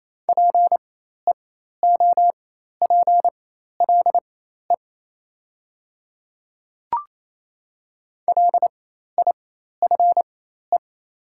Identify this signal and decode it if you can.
Morse code, heard as a keyed tone, about 28 wpm, spaced as for 13 wpm, 700 Hz. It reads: PEOPLE LIFE
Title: 4x Character Spacing - Top 300 Words - 28wpm